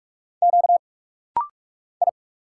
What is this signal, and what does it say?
Morse code, heard as a keyed tone, about 46 wpm, 700 Hz. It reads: Q I
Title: Single Letters - Rapid-Fire 45wpm